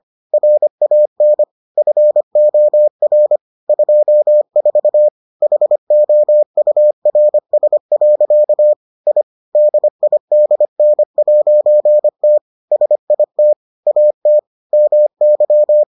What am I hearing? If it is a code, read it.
Morse code, heard as a keyed tone, about 25 wpm, 600 Hz. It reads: RAN FOR 24 HOURS. I DIDN'T SIT AT MY